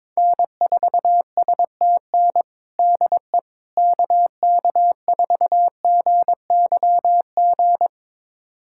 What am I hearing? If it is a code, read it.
Morse code, heard as a keyed tone, about 22 wpm, 700 Hz. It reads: N4STN DE KK4GYG
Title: USA Callsign DE Callsign 22wpm